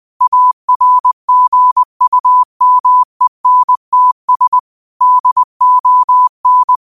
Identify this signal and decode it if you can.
Morse code, heard as a keyed tone, about 20 wpm, 1000 Hz. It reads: ARGUMENTS DON
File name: qotdfiles / 20wpm